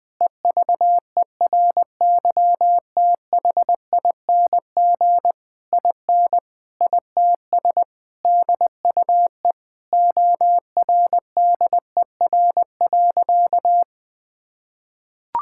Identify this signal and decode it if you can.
Morse code, heard as a keyed tone, about 21 wpm, 700 Hz. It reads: EVERYTHING IN ITS DUE ORDER.
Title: Sentences from Top 1000 Words - Review Part 6 - 20wpm